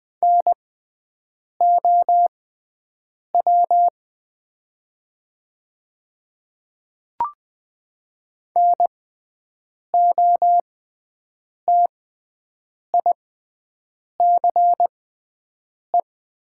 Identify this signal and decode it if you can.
Morse code, heard as a keyed tone, about 20 wpm, spaced as for 7 wpm, 700 Hz. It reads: NOW NOTICE
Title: 6x Character Spacing - Top 500 Words - 20wpm